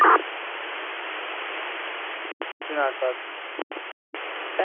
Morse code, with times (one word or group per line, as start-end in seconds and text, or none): none